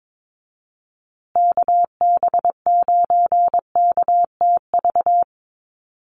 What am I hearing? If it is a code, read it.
Morse code, heard as a keyed tone, about 22 wpm, 700 Hz. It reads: KB9KTV
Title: USA Call Signs Repeated 2x 22wpm